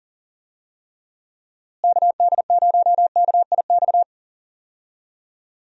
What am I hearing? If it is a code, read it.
Morse code, heard as a keyed tone, about 38 wpm, 700 Hz. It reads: KD0KIX
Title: USA Call Signs - 6 Characters - Repeated 3x - 40wpm